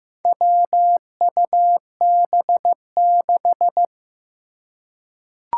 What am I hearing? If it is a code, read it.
Morse code, heard as a keyed tone, about 15 wpm, 700 Hz. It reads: WUB6
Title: Four Challenging Letter-Number Combinations 15wpm